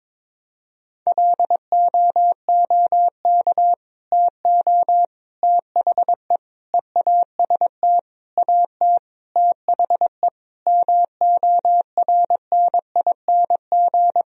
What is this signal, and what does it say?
Morse code, heard as a keyed tone, about 22 wpm, 700 Hz. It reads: LOOK TO THE EAST AT THE MORNING
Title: Sentences from Top 1000 Words - Review Part 3 - 22wpm